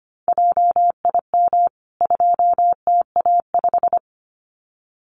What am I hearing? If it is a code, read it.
Morse code, heard as a keyed tone, about 25 wpm, 700 Hz. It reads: JIM 2TA5
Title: CWT Exchange 25wpm